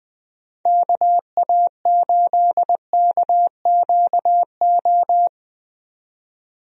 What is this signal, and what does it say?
Morse code, heard as a keyed tone, about 20 wpm, 700 Hz. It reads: KA8KQO
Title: USA Call Signs - 6 Characters - 20wpm